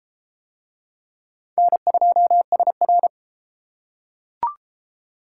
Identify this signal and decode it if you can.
Morse code, heard as a keyed tone, about 33 wpm, 700 Hz. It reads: N2SR